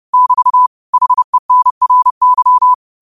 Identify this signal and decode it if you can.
Morse code, heard as a keyed tone, about 30 wpm, 1000 Hz. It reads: <BT> HENRY
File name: qotdfiles / 30wpm